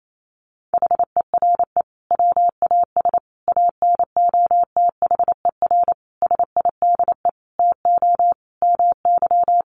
Morse code, heard as a keyed tone, about 28 wpm, 700 Hz. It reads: HERE WAS ANOTHER SIDE TO MY
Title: Sentences from Top 1500 Words 28wpm